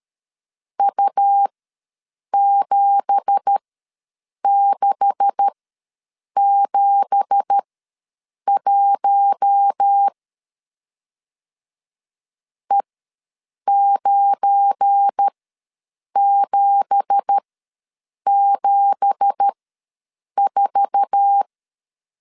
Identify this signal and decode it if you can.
Morse code, heard as a keyed tone, about 13 wpm, spaced as for 7 wpm, 800 Hz. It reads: U7671 E9774
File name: LEZIONI / 9 2867 ws 12wpm 3431